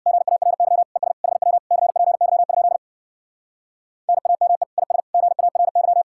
Morse code, heard as a keyed tone, about 60 wpm, 700 Hz. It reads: GARY ES HW CPY? NAME IS GARY
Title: Speed-Racing Common QSO Phrases - 50wpm